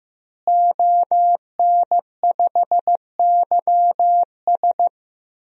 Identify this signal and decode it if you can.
Morse code, heard as a keyed tone, about 15 wpm, 700 Hz. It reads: ON5YS
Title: DXCC-100 Call Signs - 5 Characters - Repeated 3x - 15wpm